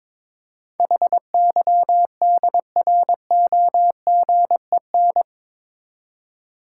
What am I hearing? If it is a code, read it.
Morse code, heard as a keyed tone, about 22 wpm, 700 Hz. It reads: HYDROGEN
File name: practice / 22wpm / Eight Letter Words - 22wpm